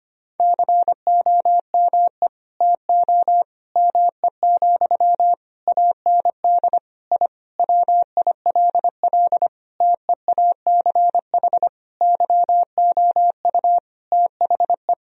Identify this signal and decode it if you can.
Morse code, heard as a keyed tone, about 25 wpm, 700 Hz. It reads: COME TO ME, AND I WILL TEACH YOU THE
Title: Sentences from Top 1000 Words - Review Part 2 - 25wpm